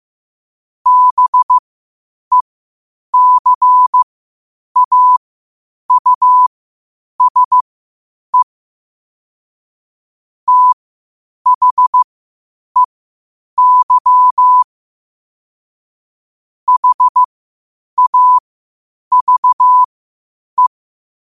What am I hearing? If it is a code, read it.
Morse code, heard as a keyed tone, about 15 wpm, spaced as for 8 wpm, 1000 Hz. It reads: BECAUSE THEY HAVE
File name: qotdfiles / 8wpm